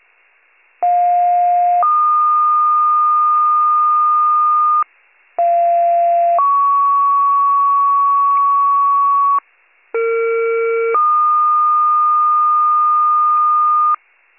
Two-tone page, heard as a one-second tone, then a three-second tone, about 710 then 1220 Hz, about 690 then 1120 Hz, about 460 then 1220 Hz. It is heard from 0.8 to 13.9 s.